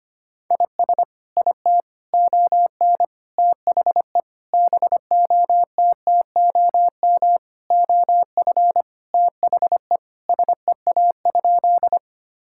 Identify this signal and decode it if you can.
Morse code, heard as a keyed tone, about 25 wpm, 700 Hz. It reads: IS IT ON THE BOTTOM OF THE SEA?
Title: Sentences from Top 1000 Words - Review Part 6 - 25wpm